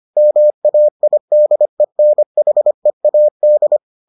Morse code, heard as a keyed tone, about 25 wpm, 600 Hz. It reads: MAIDENHEAD